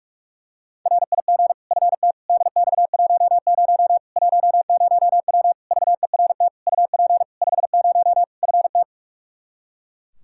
Morse code, heard as a keyed tone, about 42 wpm, 700 Hz. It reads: RIG FT DX10 10W VERT UP 50 FT
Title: Practice QSO - Exchange 2 - Info - 45wpm